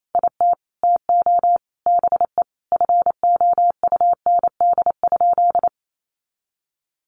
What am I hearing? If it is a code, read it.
Morse code, heard as a keyed tone, about 28 wpm, 700 Hz. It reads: IT TO BE FOUND?
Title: Sentences from Top 300 Words 28wpm